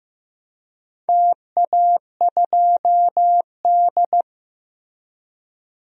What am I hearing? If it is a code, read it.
Morse code, heard as a keyed tone, about 15 wpm, 700 Hz. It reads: TA2D
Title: DXCC-100 Call Signs - 4 Characters - Repeated 3x - 15wpm